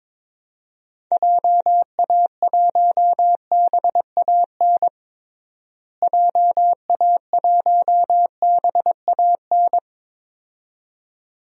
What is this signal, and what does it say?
Morse code, heard as a keyed tone, about 22 wpm, 700 Hz. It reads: JA1BAN JA1BAN
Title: DXCC-100 Call Signs - 6 Characters - Repeated 2x - 22wpm